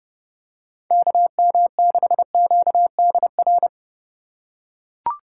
Morse code, heard as a keyed tone, about 28 wpm, 700 Hz. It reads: KM6QDR